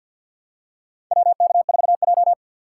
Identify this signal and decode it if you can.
Morse code, heard as a keyed tone, about 48 wpm, 700 Hz. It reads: WK4J
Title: USA Call Signs - 4x Character Spacing - 50wpm